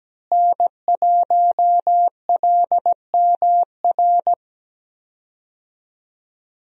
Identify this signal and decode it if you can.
Morse code, heard as a keyed tone, about 17 wpm, 700 Hz. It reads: N1LMR
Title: USA Call Signs - 5 Characters - Repeated 2x - 17wpm